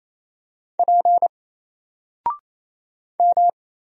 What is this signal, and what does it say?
Morse code, heard as a keyed tone, about 28 wpm, 700 Hz. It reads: P M
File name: Single-Letter-Number-28wpm